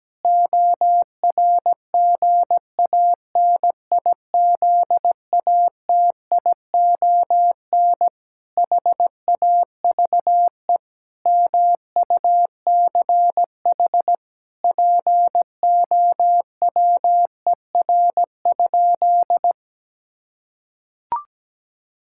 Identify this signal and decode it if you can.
Morse code, heard as a keyed tone, about 17 wpm, 700 Hz. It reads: ORGANIZATION HAVE MUCH POWER?
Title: Sentences from Top 1000 Words - Review Part 5 - 17wpm